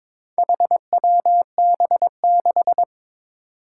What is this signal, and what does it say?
Morse code, heard as a keyed tone, about 22 wpm, 700 Hz. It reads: HWB6
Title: Four Challenging Letter-Number Combinations 22wpm